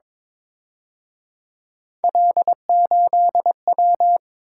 Morse code, heard as a keyed tone, about 26 wpm, 700 Hz. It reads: L8W
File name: Speed-Racing - Three Letter-Number Combinations - 22wpm